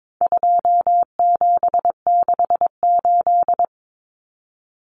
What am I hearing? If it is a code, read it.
Morse code, heard as a keyed tone, about 22 wpm, 700 Hz. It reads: 2768
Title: Common Cut Numbers - 4-Digit 22wpm